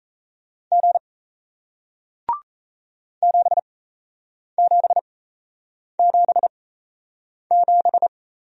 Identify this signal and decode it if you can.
Morse code, heard as a keyed tone, about 38 wpm, spaced as for 8 wpm, 700 Hz. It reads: G 7777